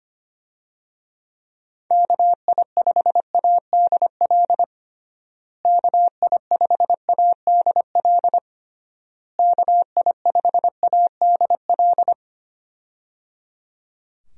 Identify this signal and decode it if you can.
Morse code, heard as a keyed tone, about 25 wpm, 700 Hz. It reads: KI5ADL KI5ADL KI5ADL
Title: USA Call Signs - 6 Characters - Repeated 3x - 25wpm